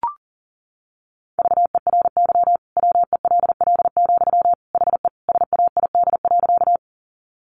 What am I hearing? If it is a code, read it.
Morse code, heard as a keyed tone, about 40 wpm, 700 Hz. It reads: VERY WELL, HE SAID.
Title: Sentences from Top 200 Words 40wpm